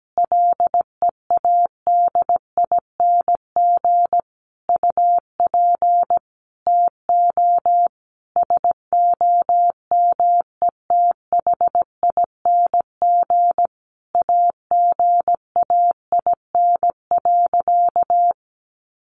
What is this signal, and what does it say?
Morse code, heard as a keyed tone, about 17 wpm, 700 Hz. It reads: LEADING UP TO SOMETHING AGAIN.
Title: Sentences from Top 1100 Words 17wpm